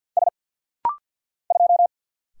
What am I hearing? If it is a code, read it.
Morse code, heard as a keyed tone, about 50 wpm, 700 Hz. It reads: S 2
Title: Single Letter-Number Alt - Rapid-Fire 50wpm